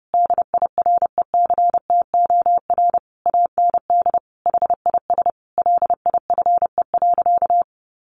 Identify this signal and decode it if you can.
Morse code, heard as a keyed tone, about 30 wpm, 700 Hz. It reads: DIRECTOR AND HIS LIFE.
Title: Sentences from Top 1300 Words 30wpm